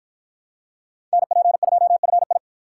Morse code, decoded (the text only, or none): NP2FI